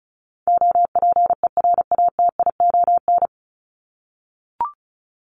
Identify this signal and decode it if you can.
Morse code, heard as a keyed tone, about 34 wpm, 700 Hz. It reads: OPERATION
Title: Top 1501-1750 Words 35wpm